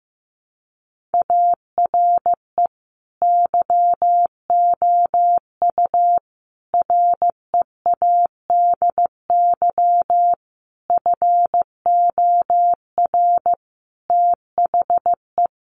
Morse code, heard as a keyed tone, about 15 wpm, 700 Hz. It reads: ARE YOU READY FOR THE
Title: Sentences from Top 1500 Words 15wpm